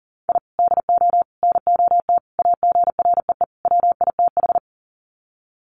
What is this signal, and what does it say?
Morse code, heard as a keyed tone, about 40 wpm, 700 Hz. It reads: I DO NOT AGREE WITH
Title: Sets of 5 Words Encore 40wpm